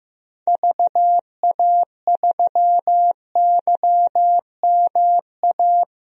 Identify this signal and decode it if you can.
Morse code, heard as a keyed tone, about 15 wpm, 700 Hz. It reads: VA3YMA
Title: DXCC-100 Call Signs - 6 Characters - Repeated 2x - 15wpm